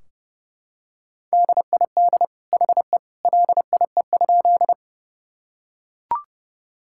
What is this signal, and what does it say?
Morse code, heard as a keyed tone, about 30 wpm, 700 Hz. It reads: DID HE LIE?
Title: Sentences from Top 1600 Words 30wpm